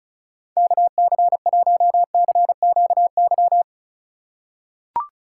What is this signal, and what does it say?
Morse code, heard as a keyed tone, about 34 wpm, 700 Hz. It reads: KC1CQY
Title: USA Call Signs - 6 Characters - 3x Character Spacing - 35wpm